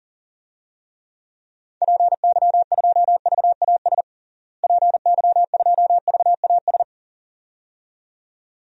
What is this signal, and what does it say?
Morse code, heard as a keyed tone, about 40 wpm, 700 Hz. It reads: PY2VAS PY2VAS
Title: DXCC-100 Call Signs - 6 Characters - Repeated 2x - 40wpm